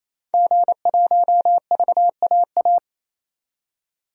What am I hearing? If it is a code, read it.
Morse code, heard as a keyed tone, about 28 wpm, 700 Hz. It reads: G1VAA
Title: DXCC-100 Call Signs - 5 Characters - 28wpm